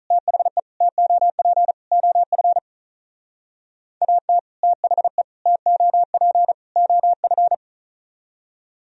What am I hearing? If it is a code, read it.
Morse code, heard as a keyed tone, about 45 wpm, 700 Hz. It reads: THE TOP OF AT THE TOP OF